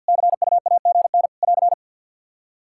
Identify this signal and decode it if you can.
Morse code, heard as a keyed tone, about 48 wpm, 700 Hz. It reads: CUAGN <AR>
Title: Practice QSO - Exchange 4 - Ending - 50wpm